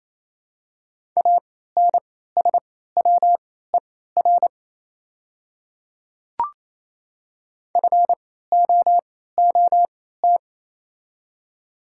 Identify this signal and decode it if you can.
Morse code, heard as a keyed tone, about 28 wpm, spaced as for 16 wpm, 690 Hz. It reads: ANSWER FOOT
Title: 3x Character Spacing - Top 500 Words - 28wpm